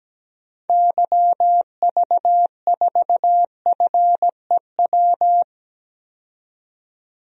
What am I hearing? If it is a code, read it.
Morse code, heard as a keyed tone, about 17 wpm, 700 Hz. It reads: YV4FEW
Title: DXCC-100 Call Signs - 6 Characters - 17wpm